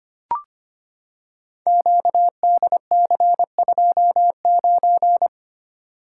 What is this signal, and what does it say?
Morse code, heard as a keyed tone, about 25 wpm, 700 Hz. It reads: QDC29
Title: Code Groups - 5 Letter:Numbers - 25wpm